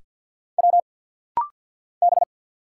Morse code, heard as a keyed tone, about 50 wpm, 700 Hz. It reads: W B
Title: Single Letters - Rapid-Fire 50wpm